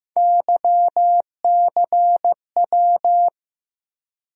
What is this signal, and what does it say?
Morse code, heard as a keyed tone, about 15 wpm, 700 Hz. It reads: YCW
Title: USA Call Sign Postfixes - 3 Letters 15wpm